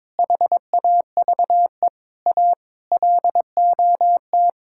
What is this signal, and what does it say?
Morse code, heard as a keyed tone, about 22 wpm, 700 Hz. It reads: HAVE A LOT
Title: Sentences from Top 1000 Words - Review Part 6 - 22wpm